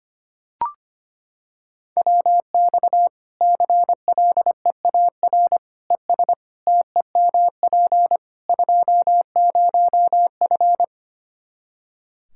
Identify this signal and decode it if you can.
Morse code, heard as a keyed tone, about 25 wpm, 700 Hz. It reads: WX CLEAR ES TEMP 20F
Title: Practice QSO - Exchange 2 - Info - 25wpm